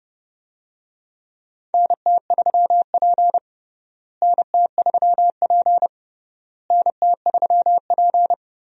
Morse code, heard as a keyed tone, about 30 wpm, 700 Hz. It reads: NT3P NT3P NT3P
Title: USA Call Signs - 4 Characters - Repeated 3x - 30wpm